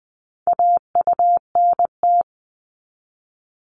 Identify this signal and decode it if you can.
Morse code, heard as a keyed tone, about 20 wpm, 700 Hz. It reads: AUNT
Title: Multi-letter Phonemes 20wpm